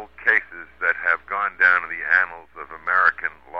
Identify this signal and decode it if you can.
Emergency Alert System (SAME) tones, heard as harsh buzzing bursts, with speech. It is heard from 0.8 to 3.4 s.